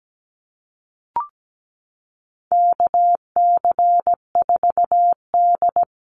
Morse code, heard as a keyed tone, about 17 wpm, 700 Hz. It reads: KC4D
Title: Call Signs Repeated 3x - 17wpm